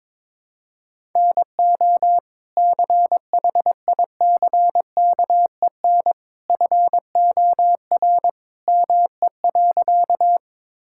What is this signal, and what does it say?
Morse code, heard as a keyed tone, about 21 wpm, 700 Hz. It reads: NO CHICKEN FOR ME.
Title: Sentences from Top 1000 Words - Review Part 5 - 22wpm